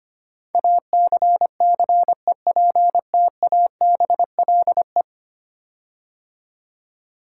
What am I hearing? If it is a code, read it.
Morse code, heard as a keyed tone, about 25 wpm, 700 Hz. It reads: ACCEPTABLE
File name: 10 Letter Words - First 3 Letters Given - 25wpm